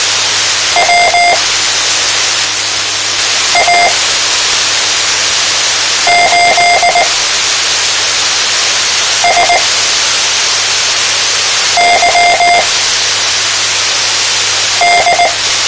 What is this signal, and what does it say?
Morse code, heard as a keyed tone, about 19 wpm, spaced as for 4 wpm, 700 Hz, with radio noise. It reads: WA8SYD
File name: tmranets / Elmer 04 19 2015